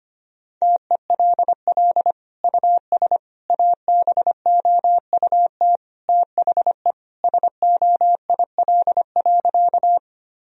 Morse code, heard as a keyed tone, about 25 wpm, 700 Hz. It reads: TELL US ABOUT THE SOIL.
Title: Sentences from Top 1000 Words - Review Part 4 - 25wpm